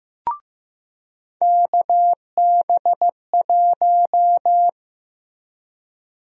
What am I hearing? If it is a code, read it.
Morse code, heard as a keyed tone, about 15 wpm, 700 Hz. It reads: KB1
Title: USA Call Sign Prefixes - Repeated 2x - 15wpm